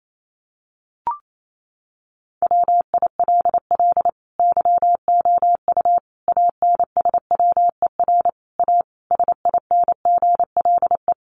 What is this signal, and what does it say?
Morse code, heard as a keyed tone, about 28 wpm, 700 Hz. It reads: WILL YOU ANSWER A SINGLE
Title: Sentences from Top 700 Words 28wpm